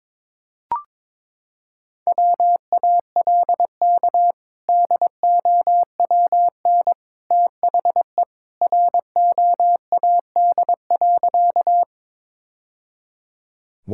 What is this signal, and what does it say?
Morse code, heard as a keyed tone, about 22 wpm, 700 Hz. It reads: WALK DOWN THE ROAD.
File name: practice / 22wpm / Sentences with 2-Word Phrases - 22wpm